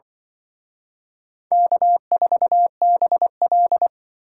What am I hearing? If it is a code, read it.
Morse code, heard as a keyed tone, about 24 wpm, 700 Hz. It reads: K4BL
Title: Speed Racing - USA 1x2 Call Signs 20wpm